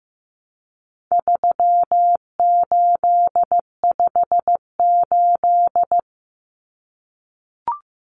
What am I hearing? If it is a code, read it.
Morse code, heard as a keyed tone, about 15 wpm, 700 Hz. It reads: 3858
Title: Common Cut Numbers - 4-Digit 15wpm